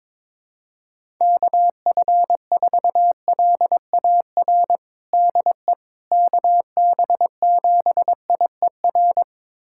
Morse code, heard as a keyed tone, about 21 wpm, 700 Hz. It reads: KF4LAR DE KB7IER
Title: USA Callsign DE Callsign - 6 Characters - 22wpm